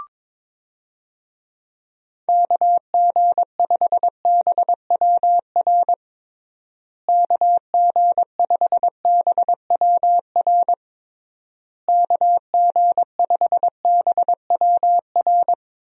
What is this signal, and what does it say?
Morse code, heard as a keyed tone, about 22 wpm, 700 Hz. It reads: KG5BWR KG5BWR KG5BWR
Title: USA Call Signs - 6 Characters - Repeated 3x - 22wpm